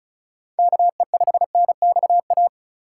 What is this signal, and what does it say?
Morse code, heard as a keyed tone, about 34 wpm, 700 Hz. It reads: KE5NXA
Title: USA Call Signs - 4x Character Spacing - 35wpm